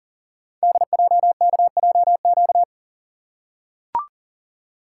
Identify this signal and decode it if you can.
Morse code, heard as a keyed tone, about 42 wpm, 690 Hz. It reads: DJKJQ
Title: Code Groups - 5 Letters - 40wpm